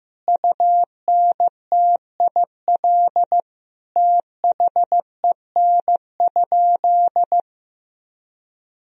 Morse code, heard as a keyed tone, about 15 wpm, 700 Hz. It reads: UNTIL THEN?
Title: Sentences from Top 500 Words and Abbreviations 15wpm